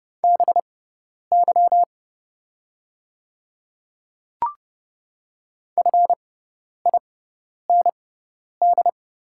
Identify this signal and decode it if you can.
Morse code, heard as a keyed tone, about 30 wpm, spaced as for 10 wpm, 700 Hz. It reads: BY FIND
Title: 6x Character Spacing - Top 200 Words - 30wpm